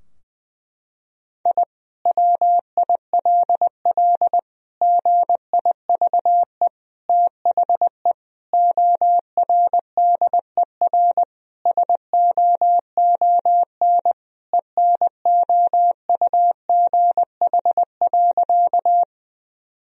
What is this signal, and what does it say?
Morse code, heard as a keyed tone, about 20 wpm, 700 Hz. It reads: I WILL GIVE THE ORDER SOON ENOUGH.
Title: Sentences from Top 1000 Words - Review Part 2 - 20wpm